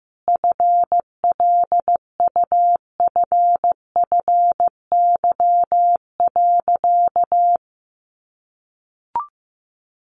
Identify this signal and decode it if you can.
Morse code, heard as a keyed tone, about 15 wpm, 700 Hz. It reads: FLUFFY.
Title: Easy Phrases 15wpm